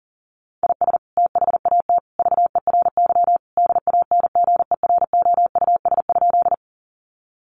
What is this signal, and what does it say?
Morse code, heard as a keyed tone, about 40 wpm, 700 Hz. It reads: IS THAT VERY DANGEROUS?
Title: Sentences from Top 1200 Words 40wpm